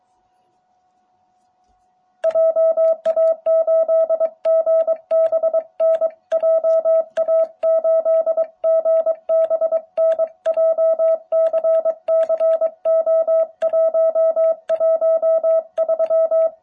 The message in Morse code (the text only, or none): JA8GBN JA8GBN JCCO113